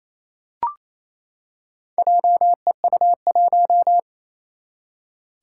Morse code, heard as a keyed tone, about 28 wpm, 700 Hz. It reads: JEU1
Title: Four Challenging Letter-Number Combinations 28wpm